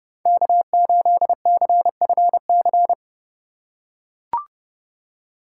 Morse code, heard as a keyed tone, about 30 wpm, 700 Hz. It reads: K8CFC
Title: USA Call Signs - 5 Characters - 3x Character Spacing - 30wpm